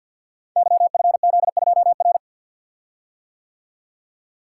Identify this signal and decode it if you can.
Morse code, heard as a keyed tone, about 46 wpm, 700 Hz. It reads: YFZ2R
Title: Code Groups - 5 Letter:Numbers - 50wpm